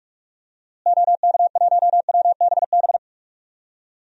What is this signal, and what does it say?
Morse code, heard as a keyed tone, about 46 wpm, 700 Hz. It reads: OK1WBB
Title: DXCC-100 Call Signs - 6 Characters - Repeated 2x - 45wpm